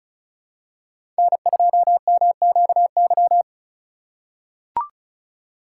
Morse code, heard as a keyed tone, about 36 wpm, 700 Hz. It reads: N2MQY
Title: USA Call Signs - 5 Characters - 2x Character Spacing - 35wpm